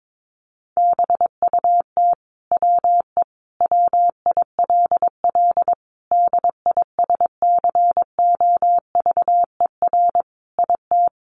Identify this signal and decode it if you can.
Morse code, heard as a keyed tone, about 22 wpm, 700 Hz. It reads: BUT WE WILL DISCOVER IT
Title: Sentences from Top 1100 Words 22wpm